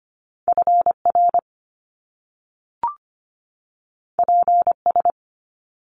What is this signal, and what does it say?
Morse code, heard as a keyed tone, about 25 wpm, 700 Hz. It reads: FR PS
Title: Top Two Letter Combinations in Words 25wpm